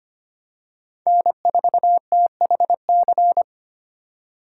Morse code, heard as a keyed tone, about 25 wpm, 700 Hz. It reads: N4THC
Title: USA Call Signs - 5 Characters - 4x Character Spacing - 25wpm